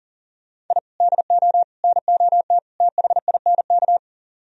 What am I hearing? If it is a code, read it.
Morse code, heard as a keyed tone, about 54 wpm, spaced as for 47 wpm, 700 Hz. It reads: I DO NOT THINK
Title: Speed-Racing - Sets of 4 Words 40wpm